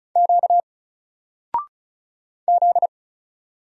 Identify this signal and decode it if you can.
Morse code, heard as a keyed tone, about 36 wpm, 700 Hz. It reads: Q Z
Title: Single Letter-Number Alt 35wpm